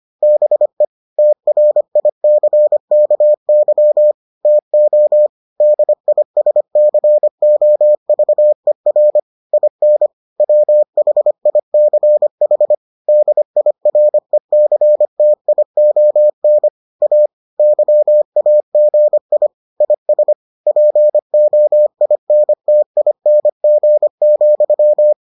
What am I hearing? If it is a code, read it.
Morse code, heard as a keyed tone, about 25 wpm, 600 Hz. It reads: BE TRICKY TO DISCOVER IN WHICH DIRECTION A YAGI IS POINTING,